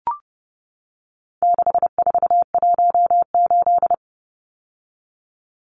Morse code, heard as a keyed tone, about 30 wpm, 700 Hz. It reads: 6418
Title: Numbers - 4-Digit 30wpm